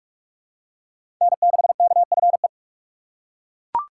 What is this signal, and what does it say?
Morse code, heard as a keyed tone, about 42 wpm, 700 Hz. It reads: N6KFE
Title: USA Call Signs - 5 Characters - 6x Character Spacing - 45wpm